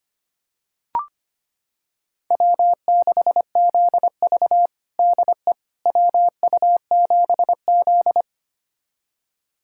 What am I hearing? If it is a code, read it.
Morse code, heard as a keyed tone, about 25 wpm, 700 Hz. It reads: W6ZV DE WU7Z
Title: USA Callsign DE Callsign - 4 Characters - 25wpm